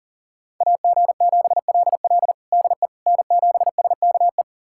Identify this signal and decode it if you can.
Morse code, heard as a keyed tone, about 40 wpm, 700 Hz. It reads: AG7LL DE N7SKE
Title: USA Callsign DE Callsign - 5 Characters - 40wpm